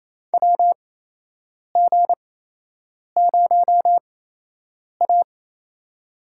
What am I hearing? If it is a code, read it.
Morse code, heard as a keyed tone, about 28 wpm, spaced as for 8 wpm, 700 Hz. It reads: WG0A
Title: USA Call Signs - 4 Characters - 8x Character Spacing - 28wpm